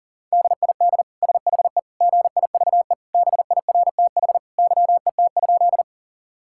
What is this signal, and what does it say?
Morse code, heard as a keyed tone, about 40 wpm, 700 Hz. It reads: DID SHE GIVE BIRTH YET?
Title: Sentences from Top 1000 Words - Review Part 5 - 40wpm